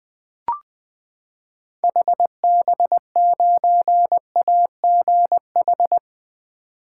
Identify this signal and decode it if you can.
Morse code, heard as a keyed tone, about 20 wpm, 700 Hz. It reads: HB9AGH
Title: 2023 SOTA Chaser Call Signs - 20wpm